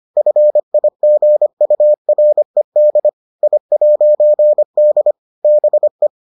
Morse code, heard as a keyed tone, about 25 wpm, 600 Hz. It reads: FIGURED I'D BE